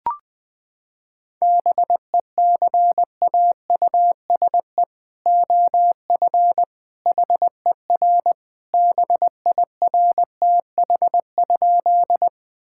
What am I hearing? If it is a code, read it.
Morse code, heard as a keyed tone, about 20 wpm, 700 Hz. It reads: BECAUSE OF HER BIRTH?
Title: Sentences from Top 1000 Words - Review Part 5 - 20wpm